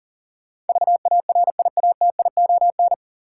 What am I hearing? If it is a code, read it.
Morse code, heard as a keyed tone, about 38 wpm, 700 Hz. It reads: VARIATION
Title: Sentences from Top 1600 Words - New Word Drill - 40wpm